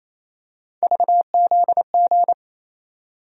Morse code, heard as a keyed tone, about 28 wpm, 700 Hz. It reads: VZG
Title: USA Call Sign Postfixes - 3 Letters 28wpm